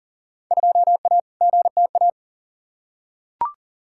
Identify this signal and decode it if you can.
Morse code, heard as a keyed tone, about 40 wpm, 700 Hz. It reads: 2A GTA
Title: Field Day Exchanges Encore - 40wpm